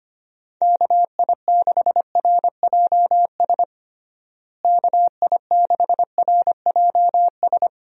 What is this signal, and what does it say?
Morse code, heard as a keyed tone, about 25 wpm, 700 Hz. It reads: KI6RJS KI6RJS
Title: USA Call Signs - 6 Characters - Repeated 2x - 25wpm